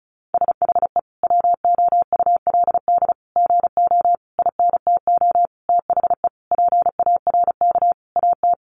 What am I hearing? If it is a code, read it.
Morse code, heard as a keyed tone, about 34 wpm, 700 Hz. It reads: SHE WOULD GO INTO THE PARK AT
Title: Sentences from Top 1300 Words 35wpm